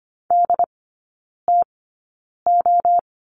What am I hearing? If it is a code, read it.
Morse code, heard as a keyed tone, about 25 wpm, spaced as for 9 wpm, 690 Hz. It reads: DTO
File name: Lesson 14e - Number 2 - Three Characters 25wpm